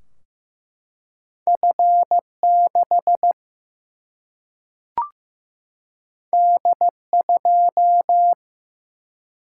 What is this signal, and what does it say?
Morse code, heard as a keyed tone, about 15 wpm, 700 Hz. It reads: F6 D2
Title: Single-Letter Single-Number 15wpm